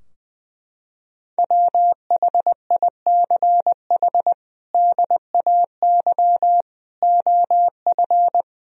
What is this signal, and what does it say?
Morse code, heard as a keyed tone, about 21 wpm, 700 Hz. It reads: WHICH DAY OF
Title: Sentences from Top 1000 Words - Review Part 3 - 20wpm